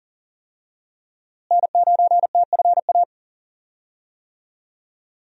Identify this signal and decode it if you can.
Morse code, heard as a keyed tone, about 40 wpm, 700 Hz. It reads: N9TFA
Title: USA Call Signs - 5 Characters - Repeated 3x - 40wpm